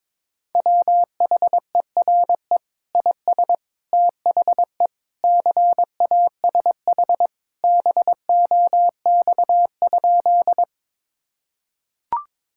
Morse code, heard as a keyed tone, about 22 wpm, 700 Hz. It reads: WHERE IS THE CASH BOX?
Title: Sentences from Top 1000 Words - Review Part 5 - 22wpm